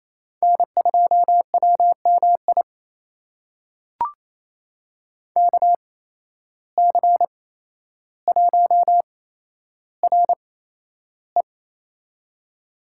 Morse code, heard as a keyed tone, about 28 wpm, spaced as for 12 wpm, 700 Hz. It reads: N2WMI KC1RE
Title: USA Call Signs - 5 Characters - 8x Character Spacing - 28wpm